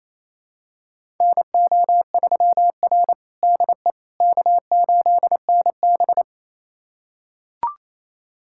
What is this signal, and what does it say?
Morse code, heard as a keyed tone, about 28 wpm, 700 Hz. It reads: NO3R DE K8NB
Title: USA Callsign DE Callsign - 4 Characters - 28wpm